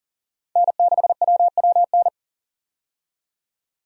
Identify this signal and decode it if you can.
Morse code, heard as a keyed tone, about 40 wpm, 700 Hz. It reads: N6WWN
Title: USA Call Signs - 5 Characters - 40wpm